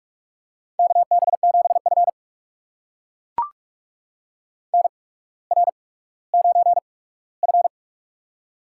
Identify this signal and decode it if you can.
Morse code, heard as a keyed tone, about 44 wpm, spaced as for 19 wpm, 700 Hz. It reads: KB7F NR9F
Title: USA Call Signs - 4 Characters - 8x Character Spacing - 45wpm